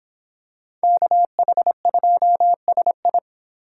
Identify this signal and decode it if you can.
Morse code, heard as a keyed tone, about 26 wpm, 700 Hz. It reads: KH2SI